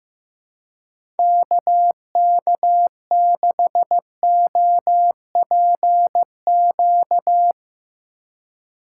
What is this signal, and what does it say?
Morse code, heard as a keyed tone, about 15 wpm, 700 Hz. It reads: KK6OPQ
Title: USA Call Signs - 6 Characters - 3x Character Spacing - 15wpm